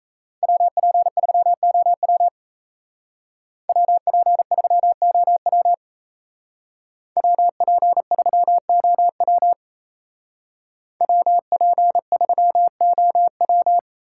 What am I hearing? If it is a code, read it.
Morse code, heard as a keyed tone, about 36 wpm, 700 Hz. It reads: WP3OW WP3OW WP3OW WP3OW